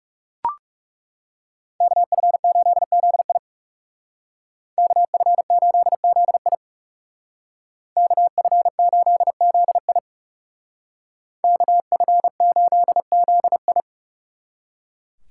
Morse code, heard as a keyed tone, about 35 wpm, 700 Hz. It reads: KF8ZI KF8ZI KF8ZI KF8ZI
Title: Speed Racing - USA Call Signs 30wpm